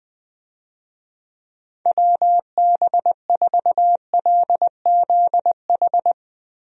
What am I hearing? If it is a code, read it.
Morse code, heard as a keyed tone, about 20 wpm, 700 Hz. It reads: WB4LZH
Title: USA Call Signs - 6 Characters - Repeated 3x - 20wpm